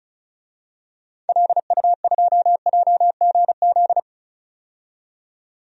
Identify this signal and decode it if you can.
Morse code, heard as a keyed tone, about 36 wpm, 700 Hz. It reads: LU2JGZ
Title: DXCC-100 Call Signs - 6 Characters - 35wpm